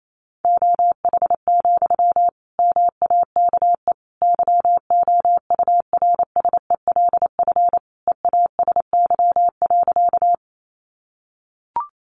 Morse code, heard as a keyed tone, about 28 wpm, 700 Hz. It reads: OH, MAKE YOURSELF EASY.
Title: Sentences from Top 700 Words 28wpm